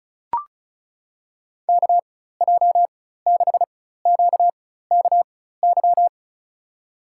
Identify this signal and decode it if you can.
Morse code, heard as a keyed tone, about 34 wpm, spaced as for 16 wpm, 700 Hz. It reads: KJ6QKY
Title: USA Call Signs - 6 Characters - 4x Character Spacing - 35wpm